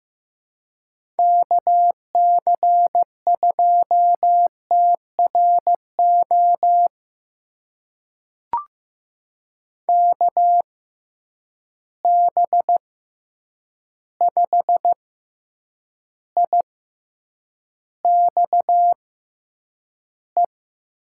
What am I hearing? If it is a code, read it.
Morse code, heard as a keyed tone, about 15 wpm, spaced as for 8 wpm, 700 Hz. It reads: KC2TRO KB5IXE
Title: USA Call Signs - 6 Characters - 6x Character Spacing - 15wpm